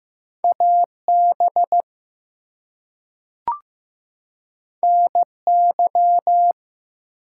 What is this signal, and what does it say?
Morse code, heard as a keyed tone, about 15 wpm, 700 Hz. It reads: AB NY
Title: US States and Canadian Provinces 15wpm